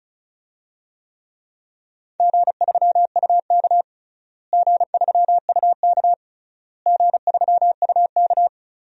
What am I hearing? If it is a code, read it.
Morse code, heard as a keyed tone, about 36 wpm, 700 Hz. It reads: G3UK G3UK G3UK
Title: DXCC-100 Call Signs - 4 Characters - Repeated 3x - 35wpm